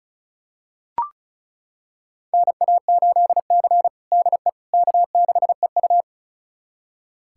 Morse code, heard as a keyed tone, about 36 wpm, 700 Hz. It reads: NA8C DE K6EU
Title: USA Callsign DE Callsign - 4 Characters - 35wpm